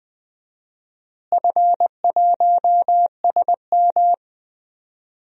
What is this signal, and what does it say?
Morse code, heard as a keyed tone, about 20 wpm, 700 Hz. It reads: F1SM
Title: DXCC-100 Call Signs - 4 Characters - Repeated 3x - 20wpm